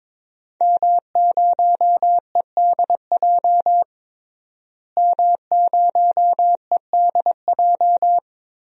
Morse code, heard as a keyed tone, about 22 wpm, 700 Hz. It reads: M0EDJ M0EDJ
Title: DXCC-100 Call Signs - 5 Characters - Repeated 3x - 22wpm